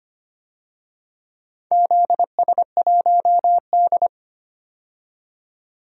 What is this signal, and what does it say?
Morse code, heard as a keyed tone, about 25 wpm, 700 Hz. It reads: ZS1D
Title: DXCC-100 Call Signs - 4 Characters - 25wpm